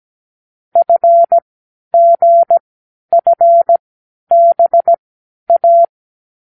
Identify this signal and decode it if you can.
Morse code, heard as a keyed tone, about 17 wpm, spaced as for 10 wpm, 680 Hz. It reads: FGFBA
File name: k7qo / Lesson12(A-H)